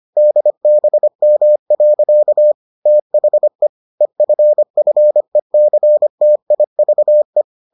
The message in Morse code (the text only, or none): DBM. THE EFFECTIVE